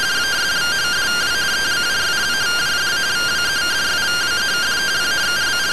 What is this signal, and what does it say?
Two-tone teleprinter signal, about 170 Hz shift.